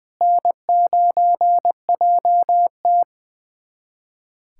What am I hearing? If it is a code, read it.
Morse code, heard as a keyed tone, about 21 wpm, 700 Hz. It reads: N9JT
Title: USA Callsign DE Callsign - 4 Characters - 20wpm